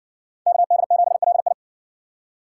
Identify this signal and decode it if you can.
Morse code, heard as a keyed tone, about 60 wpm, 700 Hz. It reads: KD7LI